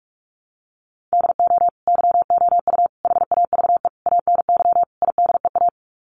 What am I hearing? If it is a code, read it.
Morse code, heard as a keyed tone, about 44 wpm, 700 Hz. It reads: DO YOU HAVE ANY IDEA
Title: Sets of 5 Words Encore 45wpm